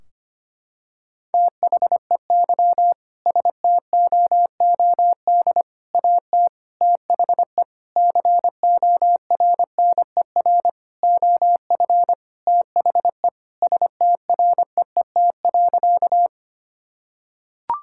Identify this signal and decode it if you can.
Morse code, heard as a keyed tone, about 25 wpm, 700 Hz. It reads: THEY STOOD AT THE CORNER OF THE STREET.
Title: Sentences from Top 1600 Words 25wpm